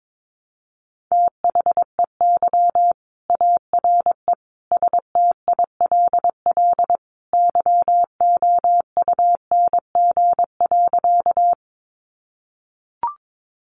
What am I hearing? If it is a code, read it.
Morse code, heard as a keyed tone, about 22 wpm, 700 Hz. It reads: THEY ARE STILL YOUNG.
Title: Sentences from Top 400 Words 22wpm